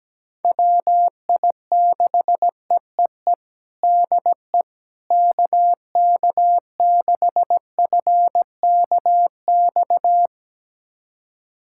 Morse code, heard as a keyed tone, about 17 wpm, 700 Hz. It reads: WI6EEE DE KK6FKX
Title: USA Callsign DE Callsign - 6 Characters - 17wpm